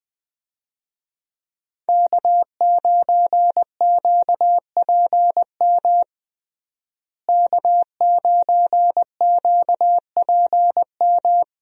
Morse code, heard as a keyed tone, about 21 wpm, 700 Hz. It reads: K9QPM K9QPM
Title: USA Call Signs - 5 Characters - Repeated 2x - 20wpm